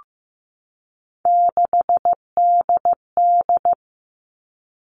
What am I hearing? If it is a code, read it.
Morse code, heard as a keyed tone, about 15 wpm, 700 Hz. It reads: 6DD
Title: All Cut Numbers - 3-Digit 15wpm